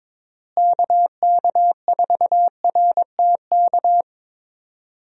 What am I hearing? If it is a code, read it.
Morse code, heard as a keyed tone, about 22 wpm, 700 Hz. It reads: KK4RTK
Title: USA Call Signs - 6 Characters - Repeated 2x - 22wpm